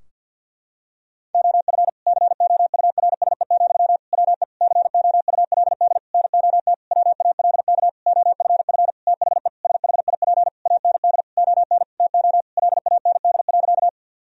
Morse code, heard as a keyed tone, about 52 wpm, 700 Hz. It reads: OF COURSE, WE COULD NOT WALK OFF THE SHIP AND ON TO LAND.
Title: Sentences from Top 1000 Words - Review Part 2 - 50wpm